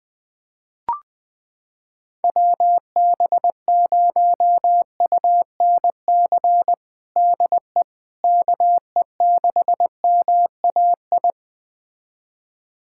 Morse code, heard as a keyed tone, about 21 wpm, 700 Hz. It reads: WB0UNC DE KE6MAI
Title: USA Callsign DE Callsign 20wpm